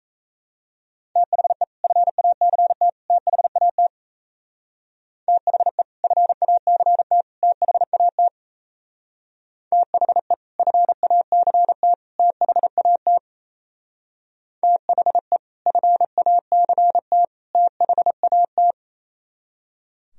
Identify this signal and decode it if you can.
Morse code, heard as a keyed tone, about 36 wpm, 700 Hz. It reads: THE FACT THAT THE FACT THAT THE FACT THAT THE FACT THAT